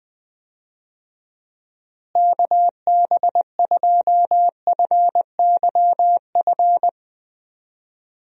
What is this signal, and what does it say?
Morse code, heard as a keyed tone, about 20 wpm, 700 Hz. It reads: KB2FYF
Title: USA Call Signs - 6 Characters - Repeated 3x - 20wpm